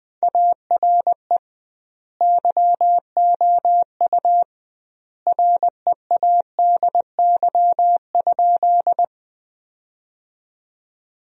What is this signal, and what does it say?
Morse code, heard as a keyed tone, about 20 wpm, 700 Hz. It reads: ARE YOU READY?
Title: 2x Word-Spacing - Sentences from Top 400 Words - 20wpm